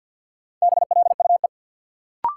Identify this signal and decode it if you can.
Morse code, heard as a keyed tone, about 51 wpm, 700 Hz. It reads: BLUE
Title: Sentences from Top 1200 Words - New Word Drill Alt - 50wpm